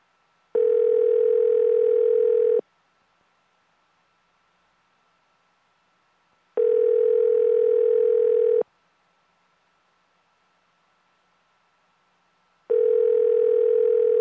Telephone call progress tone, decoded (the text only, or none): ringback tone